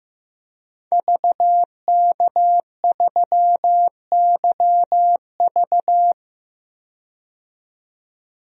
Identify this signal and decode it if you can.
Morse code, heard as a keyed tone, about 15 wpm, 700 Hz. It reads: VK3YV
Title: DXCC-100 Call Signs - 5 Characters - 15wpm